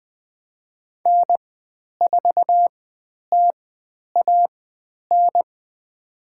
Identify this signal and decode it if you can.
Morse code, heard as a keyed tone, about 20 wpm, spaced as for 10 wpm, 700 Hz. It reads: N4TAN
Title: 095 Lesson 16g - Letter c - Call Signs 20wpm